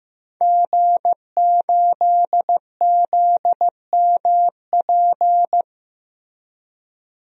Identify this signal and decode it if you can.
Morse code, heard as a keyed tone, about 15 wpm, 700 Hz. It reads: G8ZMP
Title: DXCC-100 Call Signs - 5 Characters - Repeated 3x - 15wpm